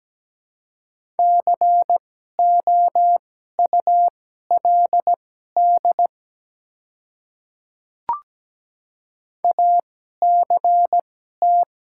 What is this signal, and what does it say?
Morse code, heard as a keyed tone, about 17 wpm, spaced as for 12 wpm, 690 Hz. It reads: COULD ACT
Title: 2x Character Spacing - Top 300 Words - 17wpm